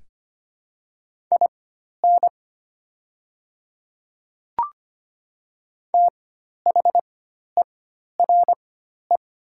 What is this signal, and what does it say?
Morse code, heard as a keyed tone, about 25 wpm, spaced as for 12 wpm, 700 Hz. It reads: IN THERE
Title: 4x Character Spacing - Top 100 Words - 25wpm